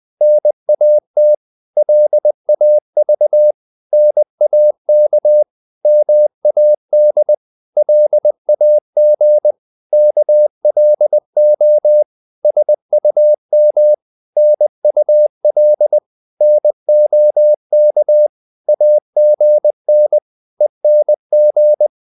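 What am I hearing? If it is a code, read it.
Morse code, heard as a keyed tone, about 20 wpm, 600 Hz. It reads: NAT LAV NAK MAD LAG KLO SUM NUL NOK AGN ENG
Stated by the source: Morse code practice of short Danish words at 20 wpm, from CW med Gnister.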